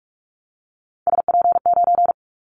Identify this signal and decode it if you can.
Morse code, heard as a keyed tone, about 44 wpm, 700 Hz. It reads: SP9
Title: Callsign Prefixes for Top 25 DXCC - Repeated 3x - 45wpm